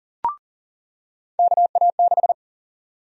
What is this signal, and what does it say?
Morse code, heard as a keyed tone, about 40 wpm, 700 Hz. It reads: KA6
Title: USA Call Sign Prefixes - 40wpm